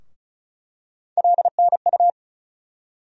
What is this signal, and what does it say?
Morse code, heard as a keyed tone, about 36 wpm, 700 Hz. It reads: LNU
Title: USA Call Sign Postfixes - 3 Letters 35wpm